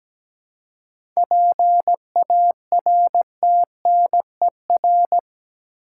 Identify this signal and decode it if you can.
Morse code, heard as a keyed tone, about 17 wpm, 700 Hz. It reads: PARTNER
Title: Seven Letter Words - 17wpm